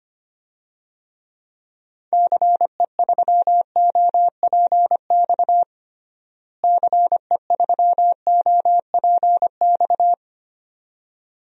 Morse code, heard as a keyed tone, about 25 wpm, 700 Hz. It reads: CE3OPX CE3OPX
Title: DXCC-100 Call Signs - 6 Characters - Repeated 2x - 25wpm